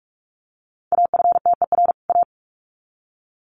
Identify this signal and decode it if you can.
Morse code, heard as a keyed tone, about 44 wpm, 700 Hz. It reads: AFTER A
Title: Sets of 2 Words 45wpm